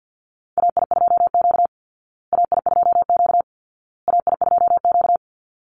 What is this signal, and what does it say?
Morse code, heard as a keyed tone, about 50 wpm, 700 Hz. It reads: AI2Q AI2Q AI2Q
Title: Call Signs Repeated 2x - 50wpm